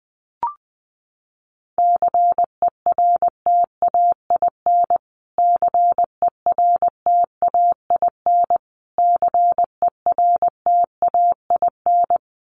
Morse code, heard as a keyed tone, about 20 wpm, 700 Hz. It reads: CERTAIN CERTAIN CERTAIN
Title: Top 500 Words Repeated 20wpm